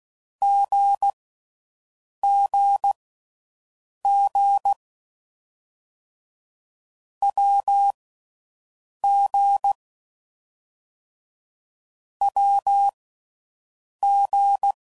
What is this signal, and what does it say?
Morse code, heard as a keyed tone, about 16 wpm, spaced as for 6 wpm, 800 Hz. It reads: GGG WG WG